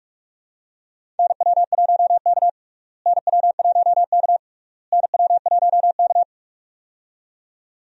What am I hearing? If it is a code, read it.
Morse code, heard as a keyed tone, about 47 wpm, 700 Hz. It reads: NW1K NW1K NW1K
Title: USA Call Signs - 4 Characters - Repeated 3x - 45wpm